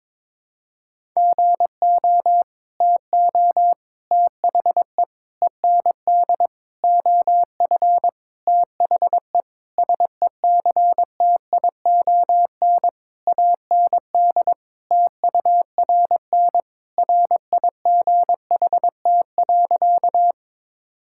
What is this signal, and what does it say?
Morse code, heard as a keyed tone, about 22 wpm, 700 Hz. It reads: GO TO THE END OF THE SECTION AND TURN RIGHT.
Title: Sentences from Top 1000 Words - Review Part 5 - 22wpm